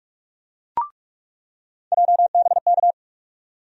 Morse code, heard as a keyed tone, about 44 wpm, 700 Hz. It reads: JBK
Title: USA Call Sign Postfixes - 3 Letters 45wpm